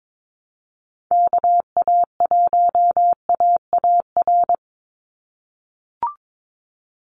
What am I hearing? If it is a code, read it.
Morse code, heard as a keyed tone, about 22 wpm, 700 Hz. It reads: KA1AAR
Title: USA Call Signs 22wpm